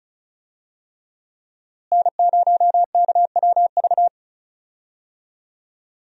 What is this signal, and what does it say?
Morse code, heard as a keyed tone, about 34 wpm, 700 Hz. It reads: N0KWV
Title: USA Call Signs - 5 Characters - Repeated 3x - 35wpm